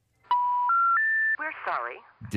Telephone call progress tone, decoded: special information tone